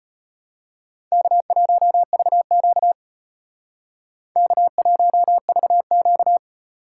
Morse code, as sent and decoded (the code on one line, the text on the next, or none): -.- .---- ...- --.- / -.- .---- ...- --.-
K1VQ K1VQ